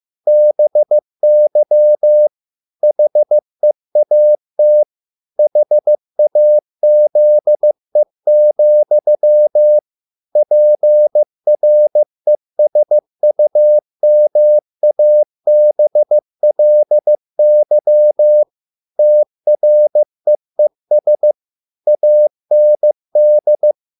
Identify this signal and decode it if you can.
Morse code, heard as a keyed tone, about 15 wpm, 600 Hz. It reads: BY HEAT HAZE, PRESUMABLY TREES AND